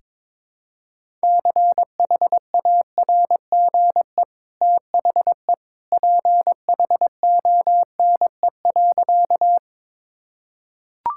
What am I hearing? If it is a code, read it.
Morse code, heard as a keyed tone, about 22 wpm, 690 Hz. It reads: CHARGE THE PHONE.
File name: practice / 22wpm / Easy Sentences with Hints - 22wpm